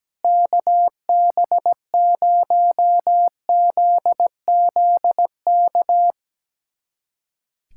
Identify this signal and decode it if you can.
Morse code, heard as a keyed tone, about 17 wpm, 700 Hz. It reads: KB0ZZK
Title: USA Callsign DE Callsign - 6 Characters - 17wpm